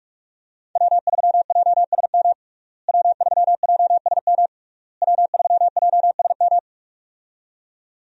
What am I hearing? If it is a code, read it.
Morse code, heard as a keyed tone, about 46 wpm, 700 Hz. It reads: W3JSM W3JSM W3JSM
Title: USA Call Signs - 5 Characters - Repeated 3x - 45wpm